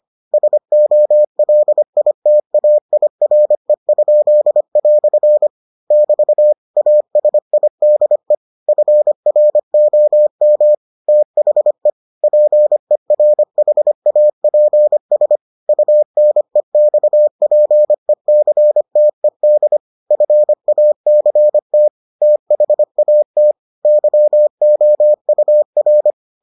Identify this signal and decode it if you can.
Morse code, heard as a keyed tone, about 25 wpm, 600 Hz. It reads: SOLITAIRE?" <BT> ASIDE FROM THE PERHAPS UNEXPECTED FACT THAT YOUR